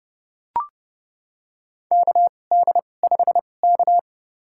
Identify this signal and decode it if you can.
Morse code, heard as a keyed tone, about 30 wpm, spaced as for 22 wpm, 700 Hz. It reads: KD5K
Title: USA Call Signs - 4 Characters - 2x Character Spacing - 30wpm